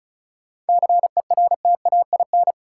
Morse code, heard as a keyed tone, about 34 wpm, 700 Hz. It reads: CERTAIN
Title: Sentences from Top 500 Words - New Word Drill - 35wpm